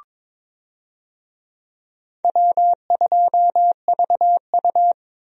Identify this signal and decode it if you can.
Morse code, heard as a keyed tone, about 22 wpm, 700 Hz. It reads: W2VU
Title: USA Call Signs - 4 Characters - Repeated 2x - 22wpm